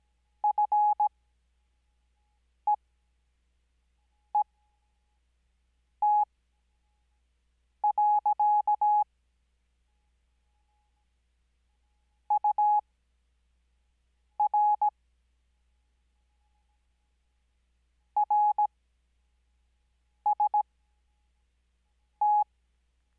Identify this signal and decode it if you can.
Morse code, heard as a keyed tone, about 17 wpm, spaced as for 5 wpm, 840 Hz. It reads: FEET. UR RST
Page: Farnsworth Spacing: 15WPM Character speed Spaced at 5WPM.